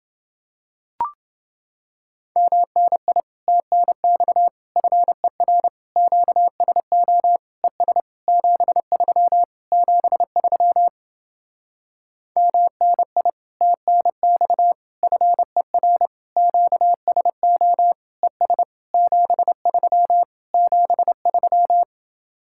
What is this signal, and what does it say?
Morse code, heard as a keyed tone, about 30 wpm, 700 Hz. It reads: MNI TNX FER QSO ES 73 73 MNI TNX FER QSO ES 73 73